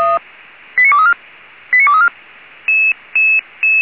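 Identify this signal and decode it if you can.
Five-tone selective call (ZVEI), heard as quick runs of five tones, tones of about 70 ms. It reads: 89234 89234
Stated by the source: Five-tone ZVEI selective call received on an SDR.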